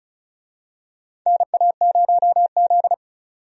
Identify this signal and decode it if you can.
Morse code, heard as a keyed tone, about 36 wpm, 700 Hz. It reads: NA0Z
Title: USA Call Signs - 4 Characters - 35wpm